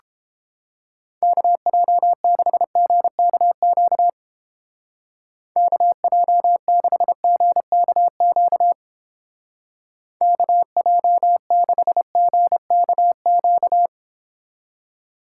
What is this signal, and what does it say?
Morse code, heard as a keyed tone, about 31 wpm, 700 Hz. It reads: KJ6GKQ KJ6GKQ KJ6GKQ